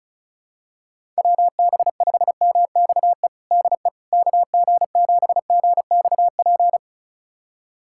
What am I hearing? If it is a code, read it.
Morse code, heard as a keyed tone, about 36 wpm, 700 Hz. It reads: WB5MXE DE KG7GXP
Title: USA Callsign DE Callsign - 6 Characters - 35wpm